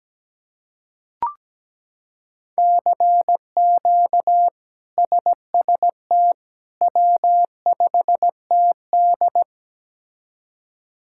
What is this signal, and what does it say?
Morse code, heard as a keyed tone, about 17 wpm, 700 Hz. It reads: CQ SST W5TD
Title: K1USN SST QSO Practice - 17wpm